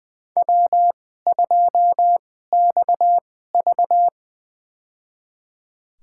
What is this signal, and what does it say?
Morse code, heard as a keyed tone, about 20 wpm, spaced as for 15 wpm, 700 Hz. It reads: W2XV
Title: USA Call Signs - 4 Characters - 2x Character Spacing - 20wpm